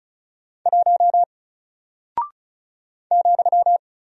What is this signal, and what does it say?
Morse code, heard as a keyed tone, about 35 wpm, 700 Hz. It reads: 1 ,
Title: Single Character (Letter:Number:Punctuation:Prosign) 35wpm